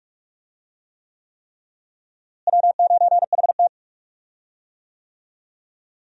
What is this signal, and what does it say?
Morse code, heard as a keyed tone, about 46 wpm, 700 Hz. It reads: W9HT
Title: USA Call Signs - 4 Characters - Repeated 3x - 45wpm